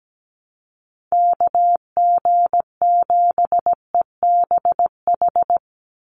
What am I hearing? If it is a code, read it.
Morse code, heard as a keyed tone, about 17 wpm, 700 Hz. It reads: KG7EBH
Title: USA Call Signs 17wpm